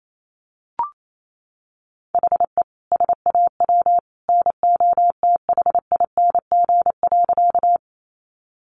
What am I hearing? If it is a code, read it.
Morse code, heard as a keyed tone, about 28 wpm, 700 Hz. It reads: HE SAW NOTHING.
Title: Sentences from Top 500 Words 28wpm